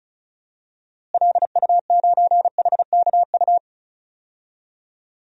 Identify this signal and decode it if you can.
Morse code, heard as a keyed tone, about 34 wpm, 700 Hz. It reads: LU9HKU
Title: DXCC-100 Call Signs - 6 Characters - Repeated 3x - 35wpm